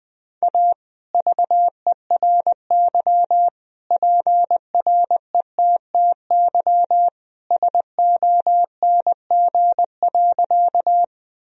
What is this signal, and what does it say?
Morse code, heard as a keyed tone, about 20 wpm, 700 Hz. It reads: A VERY PRETTY SONG.
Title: Sentences from Top 1000 Words - Review Part 2 - 20wpm